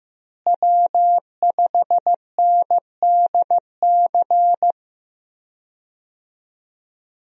Morse code, heard as a keyed tone, about 15 wpm, 700 Hz. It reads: W5NDC
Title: USA Call Signs - 5 Characters - Repeated 3x - 15wpm